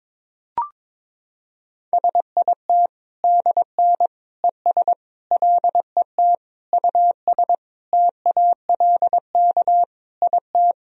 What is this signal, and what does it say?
Morse code, heard as a keyed tone, about 22 wpm, 700 Hz. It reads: SIT DN ES LET US TALK IT
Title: Sentences from Top 500 Words and Abbreviations 22wpm